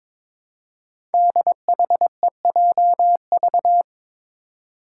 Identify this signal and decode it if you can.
Morse code, heard as a keyed tone, about 22 wpm, 700 Hz. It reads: DHEJV
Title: Five Challenging Letter-Number Combinations 22wpm